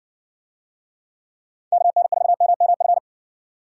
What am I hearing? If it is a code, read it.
Morse code, heard as a keyed tone, about 59 wpm, 700 Hz. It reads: KN4MMF